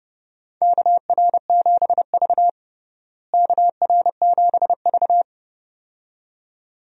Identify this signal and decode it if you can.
Morse code, heard as a keyed tone, about 32 wpm, 700 Hz. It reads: KR7V KR7V
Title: USA Call Signs - 4 Characters - Repeated 2x - 30wpm